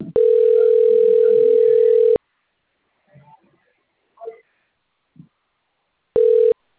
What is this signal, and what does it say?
Ringback tone (North America), 440 + 480 Hz.
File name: out- - - - - .77007